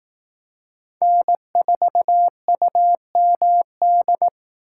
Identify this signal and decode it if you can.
Morse code, heard as a keyed tone, about 18 wpm, 700 Hz. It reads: N4UMD
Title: Speed Racing - USA Call Signs 15wpm